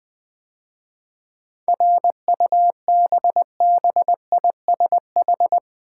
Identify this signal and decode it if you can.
Morse code, heard as a keyed tone, about 21 wpm, 700 Hz. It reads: RUBBISH
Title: 7 Letter Words - First 3 Letters Given - 20wpm